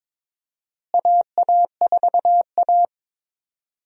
Morse code, heard as a keyed tone, about 22 wpm, 700 Hz. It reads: AA4A
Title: USA Call Signs - 4 Characters - Repeated 2x - 22wpm